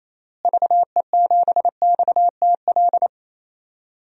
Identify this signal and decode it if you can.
Morse code, heard as a keyed tone, about 28 wpm, 700 Hz. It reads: VE7XTL
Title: DXCC-100 Call Signs - 6 Characters - Repeated 3x - 28wpm